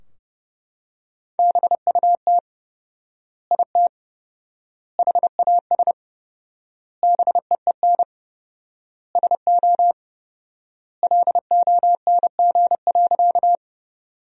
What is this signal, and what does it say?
Morse code, heard as a keyed tone, about 32 wpm, 700 Hz. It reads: BUT IT HAS BEEN SO LONG.
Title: 4x Word-Spacing - Sentences from Top 100 Words - 30wpm